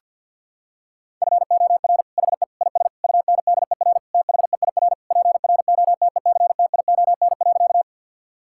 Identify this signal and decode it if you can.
Morse code, heard as a keyed tone, about 50 wpm, 700 Hz. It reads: FOR HE IS UNDER THEIR PROTECTION.
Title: Sentences from Top 1600 Words 50wpm